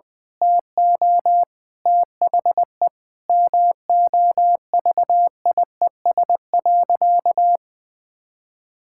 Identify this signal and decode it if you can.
Morse code, heard as a keyed tone, about 20 wpm, 700 Hz. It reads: TO THE MOVIES.
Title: Sentences with 4-Word Phrases - 20wpm